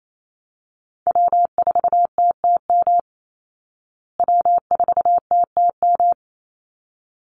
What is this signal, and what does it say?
Morse code, heard as a keyed tone, about 28 wpm, 700 Hz. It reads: W4TTM W4TTM
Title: Call Signs Repeated 3x - 28wpm